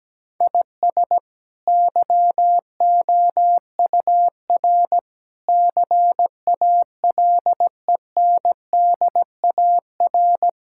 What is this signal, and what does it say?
Morse code, heard as a keyed tone, about 17 wpm, 700 Hz. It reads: IS YOUR CALENDAR
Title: Sentences from Top 1600 Words 17wpm